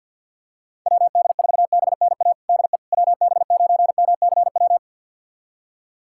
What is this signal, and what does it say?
Morse code, heard as a keyed tone, about 45 wpm, 700 Hz. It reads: WD4BNA DE WB9MCW
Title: USA Callsign DE Callsign - 6 Characters - 50wpm